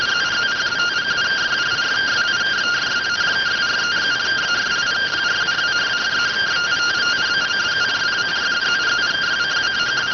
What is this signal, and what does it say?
Two-tone teleprinter signal, about 170 Hz shift.